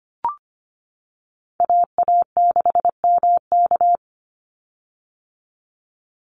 Morse code, heard as a keyed tone, about 25 wpm, 700 Hz. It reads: AA6MK
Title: Call Signs - 25wpm